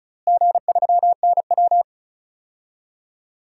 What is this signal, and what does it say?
Morse code, heard as a keyed tone, about 34 wpm, 700 Hz. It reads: G3NW
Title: DXCC-100 Call Signs - 4 Characters - Repeated 3x - 35wpm